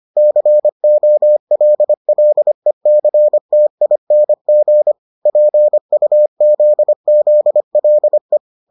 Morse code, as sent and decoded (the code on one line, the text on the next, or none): -.-. --- .-.. .-.. -.-. - .. -. --. / .--. ..- --.. --.. .-.. .
COLLECTING PUZZLE